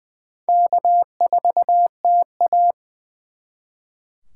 Morse code, heard as a keyed tone, about 24 wpm, 700 Hz. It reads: K4TA
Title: Speed Racing - USA 1x2 Call Signs 20wpm